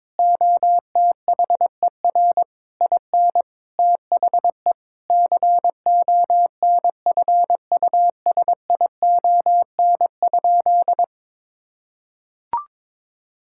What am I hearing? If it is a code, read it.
Morse code, heard as a keyed tone, about 22 wpm, 700 Hz. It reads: OTHER IN THE CONFUSION?
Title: Sentences from Top 1600 Words 22wpm